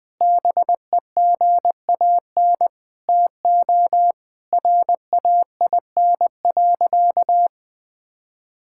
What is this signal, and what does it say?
Morse code, heard as a keyed tone, about 21 wpm, 700 Hz. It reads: BEGAN TO RAIN.
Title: Sentences with 5-Word Phrases - 20wpm